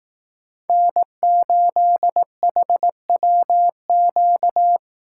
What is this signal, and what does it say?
Morse code, heard as a keyed tone, about 18 wpm, 700 Hz. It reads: N8HWQ
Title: Speed Racing - USA 1x3 Call Signs 15wpm